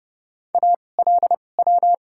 Morse code, heard as a keyed tone, about 30 wpm, spaced as for 22 wpm, 700 Hz. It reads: ALW
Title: 2x Character Spacing - Top 300 Words - 30wpm